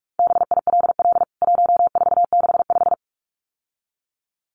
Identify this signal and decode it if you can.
Morse code, heard as a keyed tone, about 44 wpm, 700 Hz. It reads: BILL 1465
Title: CWT Exchange 45wpm